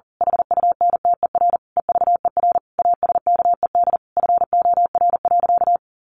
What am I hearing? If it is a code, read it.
Morse code, heard as a keyed tone, about 40 wpm, 700 Hz. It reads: HUNTER EVER ASKED FOR.
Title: A Sound Of Thunder 40wpm